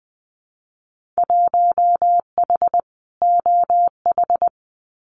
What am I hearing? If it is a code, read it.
Morse code, heard as a keyed tone, about 20 wpm, 700 Hz. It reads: 1H OH
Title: Winter Field Day Exchanges 20wpm